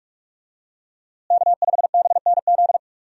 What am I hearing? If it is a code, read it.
Morse code, heard as a keyed tone, about 46 wpm, 700 Hz. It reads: K5BNZ
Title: USA Call Signs - 5 Characters - Repeated 2x - 45wpm